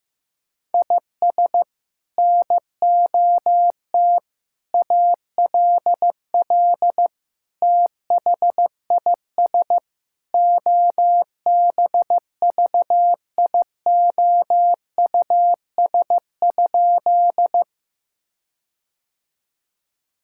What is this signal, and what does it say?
Morse code, heard as a keyed tone, about 16 wpm, 700 Hz. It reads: IS NOT ALL THIS OBVIOUS?
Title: Sentences from Top 1600 Words 15wpm